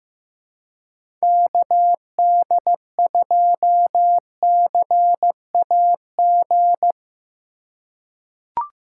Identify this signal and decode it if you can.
Morse code, heard as a keyed tone, about 15 wpm, 700 Hz. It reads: KD2CAG
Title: USA Call Signs - 6 Characters - 2x Character Spacing - 15wpm